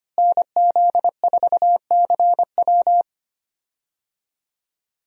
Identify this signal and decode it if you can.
Morse code, heard as a keyed tone, about 25 wpm, 700 Hz. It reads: NZ4CW
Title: USA Call Signs - 5 Characters - Repeated 2x - 25wpm